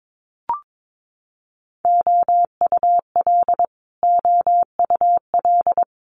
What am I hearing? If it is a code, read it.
Morse code, heard as a keyed tone, about 22 wpm, 700 Hz. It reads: OUL OUL
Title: Multi-letter Phonemes Repeated 3x 22wpm